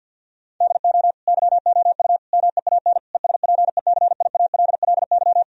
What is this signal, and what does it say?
Morse code, heard as a keyed tone, about 50 wpm, 700 Hz. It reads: DO YOU MEAN ESPECIALLY
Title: Sentences from Top 1000 Words - Review Part 4 - 50wpm